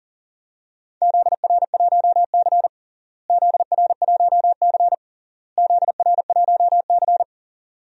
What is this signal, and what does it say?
Morse code, heard as a keyed tone, about 37 wpm, 700 Hz. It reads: ZR1C ZR1C ZR1C
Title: DXCC-100 Call Signs - 4 Characters - Repeated 3x - 40wpm